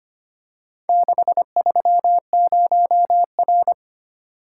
Morse code, heard as a keyed tone, about 25 wpm, 700 Hz. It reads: 630R
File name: practice / 25wpm / Four Letter-Number Combinations - 25wpm